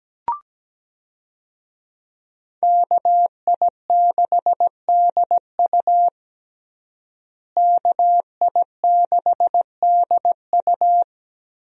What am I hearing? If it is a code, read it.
Morse code, heard as a keyed tone, about 17 wpm, 700 Hz. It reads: KI6DU KI6DU
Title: USA Call Signs - 5 Characters - Repeated 2x - 17wpm